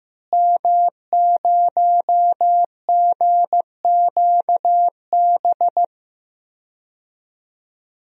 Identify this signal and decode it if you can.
Morse code, heard as a keyed tone, about 15 wpm, 700 Hz. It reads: M0GQB
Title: DXCC-100 Call Signs - 5 Characters - Repeated 3x - 15wpm